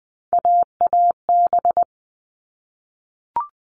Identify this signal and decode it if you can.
Morse code, heard as a keyed tone, about 20 wpm, 700 Hz. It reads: AAB
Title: All Cut Numbers - 3-Digit 20wpm